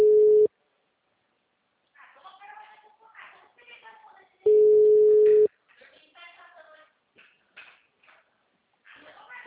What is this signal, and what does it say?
Ringback tone (Europe, 425 Hz), with speech.